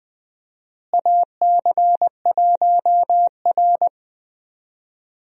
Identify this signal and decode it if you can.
Morse code, heard as a keyed tone, about 21 wpm, 690 Hz. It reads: AC1R
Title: USA Call Signs - 4 Characters - Repeated 3x - 20wpm